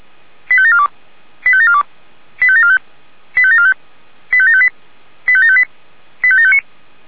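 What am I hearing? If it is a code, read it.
Five-tone selective call (ZVEI), heard as quick runs of five tones, tones of about 70 ms. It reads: 85742 85742 85745 85745 85758 85758 85759
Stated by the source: ZVEI five-tone selective call received on an SDR.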